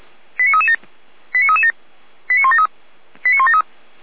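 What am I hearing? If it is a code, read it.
Five-tone selective call (ZVEI), heard as quick runs of five tones, tones of about 70 ms. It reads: 89337 89337 89173 89173
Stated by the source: ZVEI five-tone selective call received on an SDR.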